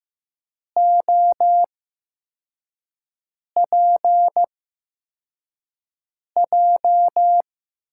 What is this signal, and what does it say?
Morse code, heard as a keyed tone, about 15 wpm, spaced as for 4 wpm, 700 Hz. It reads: OPJ
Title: USA Call Signs - 6 Characters - 8x Character Spacing - 15wpm